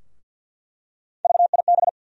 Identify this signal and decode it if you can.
Morse code, heard as a keyed tone, about 53 wpm, 700 Hz. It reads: VIB
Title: USA Call Sign Postfixes - 3 Letters 50wpm